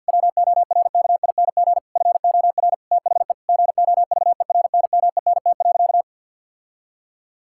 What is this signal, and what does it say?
Morse code, heard as a keyed tone, about 52 wpm, 700 Hz. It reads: WORKING FOR THE GOVERNMENT.
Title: Sentences from Top 1000 Words - Review Part 6 - 50wpm